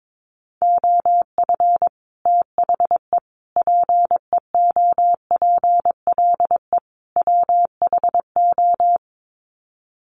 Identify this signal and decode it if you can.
Morse code, heard as a keyed tone, about 22 wpm, 700 Hz. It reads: OF THE PEOPLE WHO
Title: Sets of 4 Words 22wpm